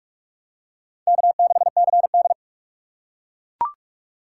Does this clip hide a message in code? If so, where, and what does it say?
Yes: Morse code from 1.1 to 2.3 s, heard as a keyed tone, about 42 wpm, 700 Hz. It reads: K6CD